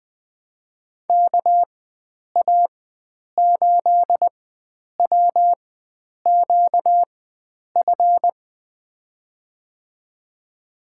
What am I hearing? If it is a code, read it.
Morse code, heard as a keyed tone, about 20 wpm, spaced as for 9 wpm, 700 Hz. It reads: KA8WQF
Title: USA Call Signs - 6 Characters - 4x Character Spacing - 20wpm